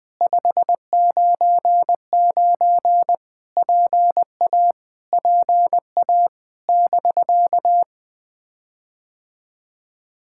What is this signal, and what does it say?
Morse code, heard as a keyed tone, about 20 wpm, 700 Hz. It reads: 599 PA PA <BK>
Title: POTA QSO Practice - 20wpm